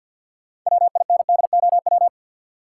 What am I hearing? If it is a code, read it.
Morse code, heard as a keyed tone, about 50 wpm, 700 Hz. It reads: WINDOW
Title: Sentences from Top 1600 Words - New Word Drill Alt - 50wpm